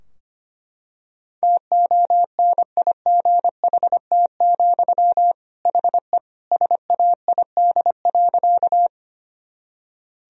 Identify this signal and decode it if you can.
Morse code, heard as a keyed tone, about 25 wpm, 700 Hz. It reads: TONIGHT, HE SAID.
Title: Sentences from Top 1000 Words - Review Part 3 - 25wpm